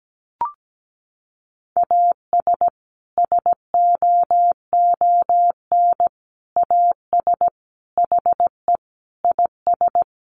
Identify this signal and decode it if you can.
Morse code, heard as a keyed tone, about 17 wpm, 700 Hz. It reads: AS SOON AS HE IS
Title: Sentences from Top 1500 Words 17wpm